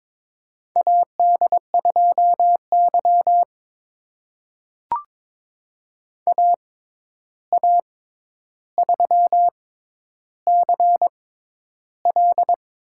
Morse code, heard as a keyed tone, about 22 wpm, spaced as for 10 wpm, 700 Hz. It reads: AD2Y AA3CL
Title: USA Call Signs - 6x Character Spacing - 22wpm